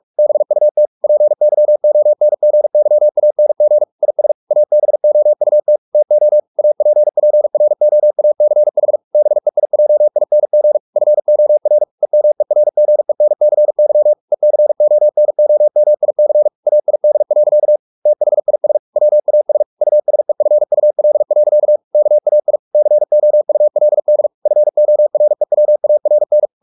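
Morse code, heard as a keyed tone, about 44 wpm, 600 Hz. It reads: BUT PYONGYANG IS ABOUT TO APPROACH BEIJING FOR EMERGENCY ECONOMIC AID. THIS WAS USEFUL. KAI COULD FOREWARN
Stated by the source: Morse practice at 45 wpm, the novel Never by Ken Follett in English, from CW med Gnister.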